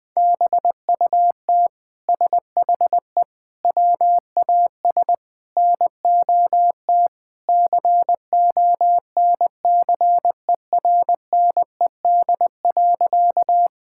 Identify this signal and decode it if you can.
Morse code, heard as a keyed tone, about 21 wpm, 700 Hz. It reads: BUT SHE WAS NOT CONCERNED.
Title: Sentences from Top 1000 Words - Review Part 6 - 20wpm